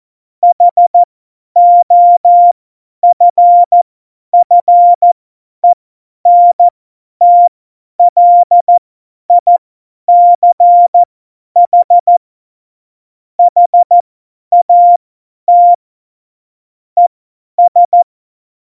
Morse code, heard as a keyed tone, about 14 wpm, spaced as for 10 wpm, 700 Hz. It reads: HOFFENTLICH HAT ES
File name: Morsekurs / Witze / Witz 11 Juli 17, Tempo 50 F70